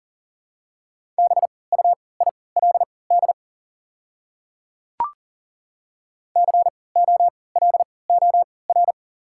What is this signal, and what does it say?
Morse code, heard as a keyed tone, about 37 wpm, spaced as for 22 wpm, 700 Hz. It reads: BUILD COLOR
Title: 3x Character Spacing - Top 300 Words - 40wpm